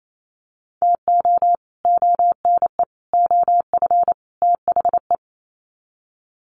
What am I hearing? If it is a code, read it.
Morse code, heard as a keyed tone, about 28 wpm, 700 Hz. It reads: TO ONE OF THE
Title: Sets of 4 Words 28wpm